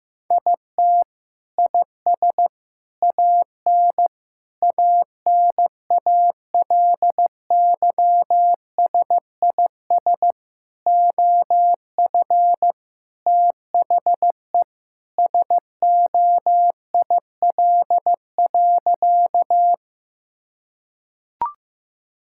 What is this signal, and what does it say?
Morse code, heard as a keyed tone, about 16 wpm, 700 Hz. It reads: IT IS AN ANALYSIS OF THE SOIL.
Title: Sentences from Top 1000 Words - Review Part 5 - 15wpm